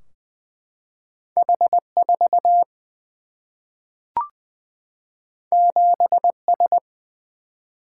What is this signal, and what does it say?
Morse code, heard as a keyed tone, about 20 wpm, 700 Hz. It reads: H 4 7 S
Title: Single-Letter Single-Number 20wpm